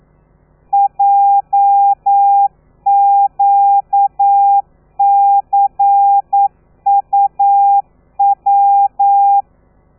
Morse code, heard as a keyed tone, about 9 wpm, 800 Hz. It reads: JQCUW
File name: telegrafia / i2tfj / Lez13 40